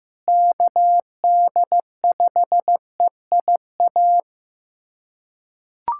Morse code, heard as a keyed tone, about 15 wpm, 700 Hz. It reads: KD5EIA
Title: USA Callsign DE Callsign 15wpm